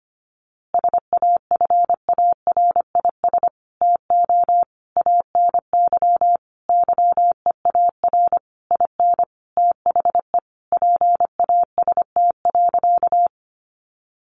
Morse code, heard as a keyed tone, about 25 wpm, 700 Hz. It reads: SAFARIS TO ANY YEAR IN THE PAST.
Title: A Sound Of Thunder 25wpm